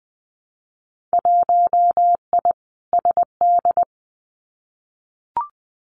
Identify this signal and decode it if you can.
Morse code, heard as a keyed tone, about 20 wpm, 700 Hz. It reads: 1I SD
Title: Winter Field Day Exchanges 20wpm